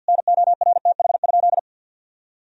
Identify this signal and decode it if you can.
Morse code, heard as a keyed tone, about 50 wpm, 700 Hz. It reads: NORTH?
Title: Sentences from Top 1000 Words - Review Part 2 - 50wpm